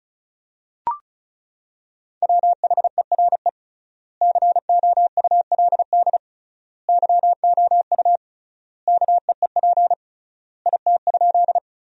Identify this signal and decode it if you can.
Morse code, heard as a keyed tone, about 34 wpm, 700 Hz. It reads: WHERE COULD YOU KEEP IT?
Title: 3x Word-Spacing - Sentences from Top 300 Words - 35wpm